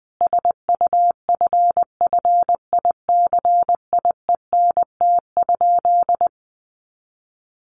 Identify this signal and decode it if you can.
Morse code, heard as a keyed tone, about 19 wpm, 700 Hz. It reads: SUFFICIENT?
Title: Sentences from Top 1400 Words 20wpm